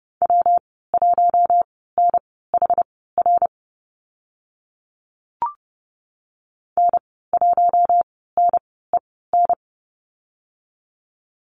Morse code, heard as a keyed tone, about 30 wpm, spaced as for 17 wpm, 700 Hz. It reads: W1NHR N1NEN
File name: Lesson 19g - Letter w - Call Signs 30wpm